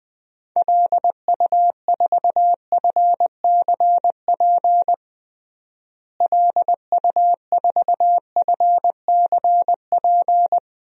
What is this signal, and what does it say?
Morse code, heard as a keyed tone, about 21 wpm, 700 Hz. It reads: LU4FCP LU4FCP
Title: DXCC-100 Call Signs - 6 Characters - Repeated 2x - 20wpm